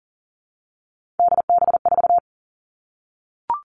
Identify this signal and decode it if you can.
Morse code, heard as a keyed tone, about 40 wpm, 700 Hz. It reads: DB4
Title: All Cut Numbers - 3-Digit 40wpm